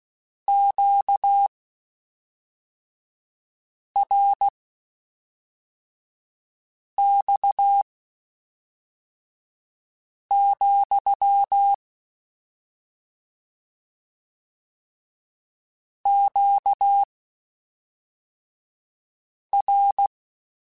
Morse code, heard as a keyed tone, about 16 wpm, spaced as for 3 wpm, 800 Hz. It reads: QRX, QR
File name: Q Calls / Q Calls-mp3s / Q Calls SPSL2 @5